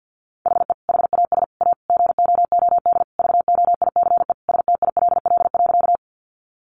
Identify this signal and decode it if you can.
Morse code, heard as a keyed tone, about 50 wpm, 700 Hz. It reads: HE HAS A GOOD VOICE STILL.
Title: Sentences from Top 500 Words 50wpm